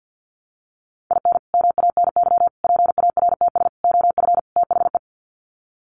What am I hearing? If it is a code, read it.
Morse code, heard as a keyed tone, about 50 wpm, 700 Hz. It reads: IN MANY PARTS OF THE
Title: Sets of 5 Words Encore 50wpm